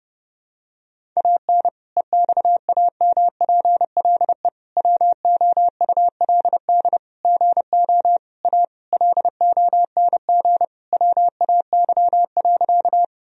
Morse code, heard as a keyed tone, about 30 wpm, 700 Hz. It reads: AN EXAMPLE WOULD GO A LONG WAY.
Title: Sentences from Top 1000 Words - Review Part 1 - 30wpm